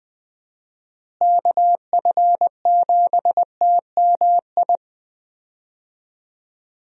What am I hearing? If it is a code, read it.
Morse code, heard as a keyed tone, about 20 wpm, 700 Hz. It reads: KF7TMI
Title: USA Call Signs - 6 Characters - Repeated 3x - 20wpm